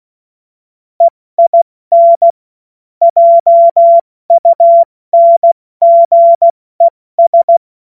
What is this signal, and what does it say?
Morse code, heard as a keyed tone, about 16 wpm, spaced as for 14 wpm, 680 Hz. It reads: EIN JUNGES
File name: Morsekurs / Witze / Witz 15. August 2023 70 F80